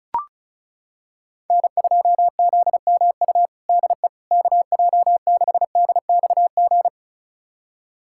Morse code, heard as a keyed tone, about 36 wpm, 700 Hz. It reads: N2ZMU DE KJ6DXG
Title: USA Callsign DE Callsign 35wpm